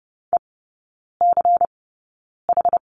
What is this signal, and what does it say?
Morse code, heard as a keyed tone, about 30 wpm, spaced as for 9 wpm, 700 Hz. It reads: ECH
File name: Lesson 19e - Letter w - Three Characters 30wpm